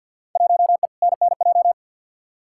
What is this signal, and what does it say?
Morse code, heard as a keyed tone, about 53 wpm, 700 Hz. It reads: 1E NNJ
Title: Field Day Exchanges - 50wpm